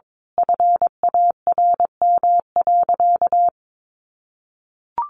Morse code, heard as a keyed tone, about 22 wpm, 700 Hz. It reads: FARM.
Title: Sentences from Top 1200 Words 22wpm